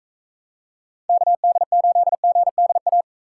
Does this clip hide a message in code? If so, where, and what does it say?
Yes: Morse code from 1.1 to 3.0 s, heard as a keyed tone, about 40 wpm, 700 Hz. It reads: KD8GDA